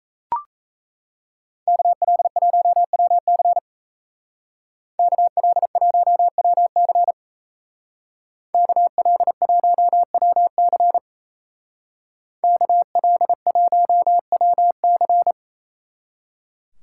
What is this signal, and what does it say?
Morse code, heard as a keyed tone, about 37 wpm, 700 Hz. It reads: KL1WC KL1WC KL1WC KL1WC